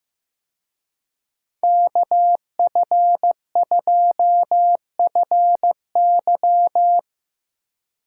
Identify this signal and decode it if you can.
Morse code, heard as a keyed tone, about 18 wpm, 700 Hz. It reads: KF2FY
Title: Speed Racing - USA 2x2 Call Signs 15wpm